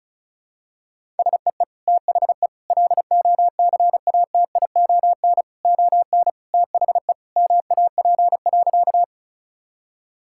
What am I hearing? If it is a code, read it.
Morse code, heard as a keyed tone, about 36 wpm, 700 Hz. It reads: SEE THE LOCATION ON THE MAP.
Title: Sentences from Top 1000 Words - Review Part 6 - 35wpm